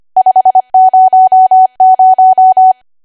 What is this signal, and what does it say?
Morse code, heard as a keyed tone, about 25 wpm, 750 Hz. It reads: 500
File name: Archive / 25 WPM / 25WPM